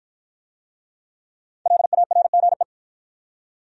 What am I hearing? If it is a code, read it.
Morse code, heard as a keyed tone, about 53 wpm, 700 Hz. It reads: LARGE